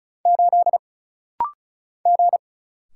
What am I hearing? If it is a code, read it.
Morse code, heard as a keyed tone, about 36 wpm, 700 Hz. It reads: 8 G
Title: Single Letter-Number - Rapid-Fire 35wpm